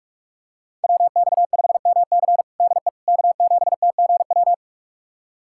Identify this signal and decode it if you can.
Morse code, heard as a keyed tone, about 41 wpm, 700 Hz. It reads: WX5MC DE K7TGW
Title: USA Callsign DE Callsign - 5 Characters - 45wpm